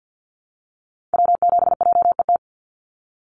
Morse code, heard as a keyed tone, about 48 wpm, 700 Hz. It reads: W7JET
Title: NA SOTA Activator Callsigns 50wpm